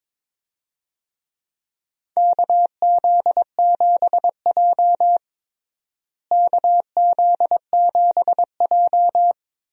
Morse code, heard as a keyed tone, about 22 wpm, 700 Hz. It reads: KZ7J KZ7J
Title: USA Call Signs - 4 Characters - Repeated 3x - 22wpm